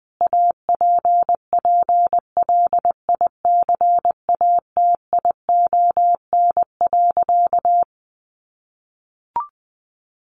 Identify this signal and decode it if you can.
Morse code, heard as a keyed tone, about 20 wpm, 700 Hz. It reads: APPLICATION.
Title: Sentences from Top 1200 Words 20wpm